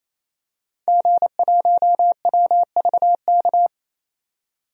Morse code, heard as a keyed tone, about 28 wpm, 700 Hz. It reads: G1WVK
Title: DXCC-100 Call Signs - 5 Characters - 28wpm